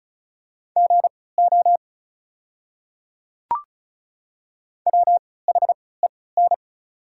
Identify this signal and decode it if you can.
Morse code, heard as a keyed tone, about 35 wpm, spaced as for 20 wpm, 690 Hz. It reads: GO WHEN
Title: 3x Character Spacing - Top 100 Words - 35wpm